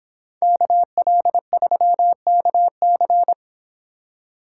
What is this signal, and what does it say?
Morse code, heard as a keyed tone, about 26 wpm, 700 Hz. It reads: KL3KC